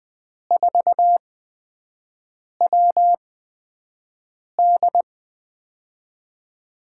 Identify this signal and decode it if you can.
Morse code, heard as a keyed tone, about 21 wpm, spaced as for 5 wpm, 700 Hz. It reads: 4WD
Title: USA Call Signs - 5 Characters - 8x Character Spacing - 20wpm